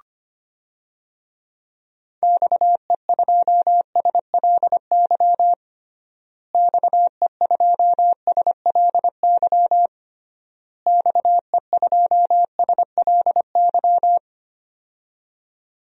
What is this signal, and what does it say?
Morse code, heard as a keyed tone, about 25 wpm, 700 Hz. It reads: XE2SLY XE2SLY XE2SLY
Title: DXCC-100 Call Signs - 6 Characters - Repeated 3x - 25wpm